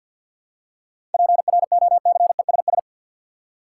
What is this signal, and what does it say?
Morse code, heard as a keyed tone, about 53 wpm, 700 Hz. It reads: PROCESS
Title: Seven Letter Words - 50wpm